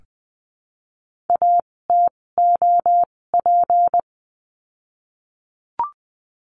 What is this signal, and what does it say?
Morse code, heard as a keyed tone, about 20 wpm, spaced as for 16 wpm, 700 Hz. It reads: ATOP
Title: 256 Lesson 42a - Words 12-20wpm